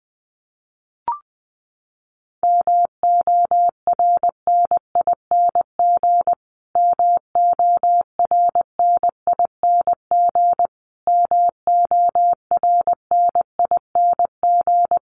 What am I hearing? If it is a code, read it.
Morse code, heard as a keyed tone, about 20 wpm, 700 Hz. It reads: MORNING MORNING MORNING
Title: Top 500 Words Repeated 20wpm